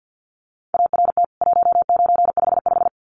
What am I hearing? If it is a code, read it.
Morse code, heard as a keyed tone, about 49 wpm, 700 Hz. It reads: ART 1955
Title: CWT Exchange 50wpm